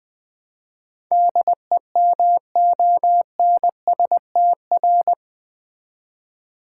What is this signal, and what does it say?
Morse code, heard as a keyed tone, about 20 wpm, 700 Hz. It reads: DEMONSTR
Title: Build A Word - 4 Letters At A Time - 11-Letter Words - 20wpm